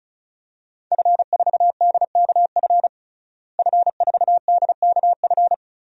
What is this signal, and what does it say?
Morse code, heard as a keyed tone, about 34 wpm, 700 Hz. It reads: F4DKF F4DKF
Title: DXCC-100 Call Signs - 5 Characters - Repeated 2x - 35wpm